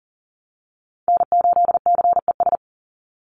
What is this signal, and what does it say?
Morse code, heard as a keyed tone, about 40 wpm, 690 Hz. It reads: N8CES
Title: USA Call Signs 40wpm